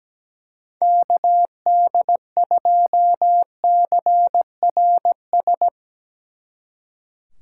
Morse code, heard as a keyed tone, about 18 wpm, 700 Hz. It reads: KD2CRS
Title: Speed Racing - USA 2x3 Call Signs 17wpm